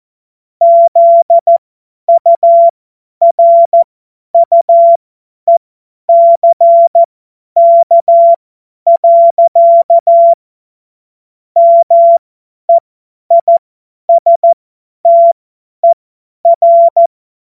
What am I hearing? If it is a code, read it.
Morse code, heard as a keyed tone, about 14 wpm, spaced as for 10 wpm, 680 Hz. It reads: ZURUECK. MEISTER
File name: Morsekurs / Witze / Witz 02. Dezember 2025 50 F70